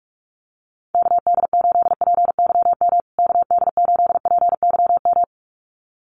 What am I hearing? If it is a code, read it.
Morse code, heard as a keyed tone, about 44 wpm, 700 Hz. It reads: KD8PYM KD8PYM
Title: USA Call Signs Repeated 2x 45wpm